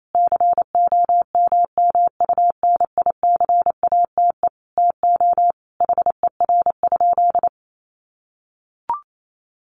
Morse code, heard as a keyed tone, about 28 wpm, 700 Hz. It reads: COMMUNICATE TO HER?
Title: Sentences from Top 1300 Words 28wpm